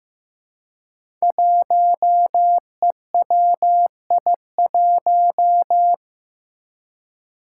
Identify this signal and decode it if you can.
Morse code, heard as a keyed tone, about 15 wpm, 700 Hz. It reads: 1EWI1
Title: Five Challenging Letter-Number Combinations 15wpm